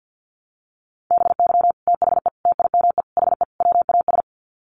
Morse code, heard as a keyed tone, about 50 wpm, 700 Hz. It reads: BY THE TIME HE WAS
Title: Sets of 5 Words Encore 50wpm